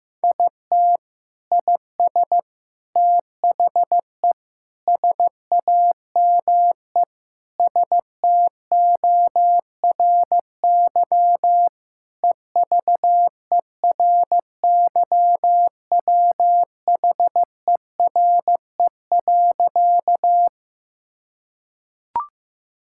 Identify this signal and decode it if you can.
Morse code, heard as a keyed tone, about 15 wpm, 700 Hz. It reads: IT IS THE SAME STORY EVERYWHERE.
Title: Sentences from Top 1600 Words 15wpm